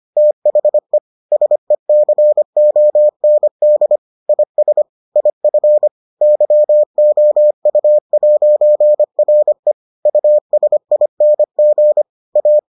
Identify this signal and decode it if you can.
Morse code, heard as a keyed tone, about 25 wpm, 600 Hz. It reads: THE SECOND IS IF YOU'RE USING A